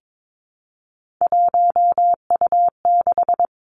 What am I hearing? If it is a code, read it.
Morse code, heard as a keyed tone, about 22 wpm, 700 Hz. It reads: 1U6
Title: Three Challenging Letter-Number Combinations 22wpm